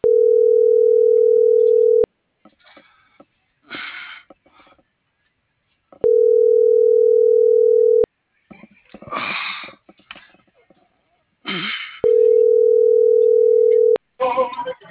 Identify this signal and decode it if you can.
Ringback tone (North America), 440 + 480 Hz.
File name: CENTRAL BLN / out hector.tellez